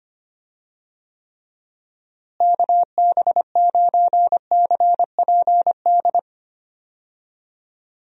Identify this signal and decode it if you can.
Morse code, heard as a keyed tone, about 25 wpm, 700 Hz. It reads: KB9CPD
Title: USA Call Signs - 6 Characters - Repeated 3x - 25wpm